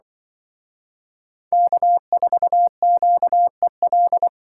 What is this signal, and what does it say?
Morse code, heard as a keyed tone, about 24 wpm, 700 Hz. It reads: K4QEL
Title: Speed Racing - USA Call Signs 20wpm